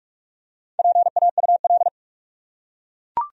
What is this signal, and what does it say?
Morse code, heard as a keyed tone, about 43 wpm, 700 Hz. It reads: PAUL
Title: USA Common Names - Max 4 Letters - 45wpm